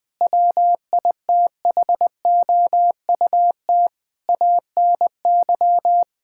Morse code, heard as a keyed tone, about 20 wpm, 700 Hz. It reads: WITHOUT ANY
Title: Sentences from Top 1600 Words 20wpm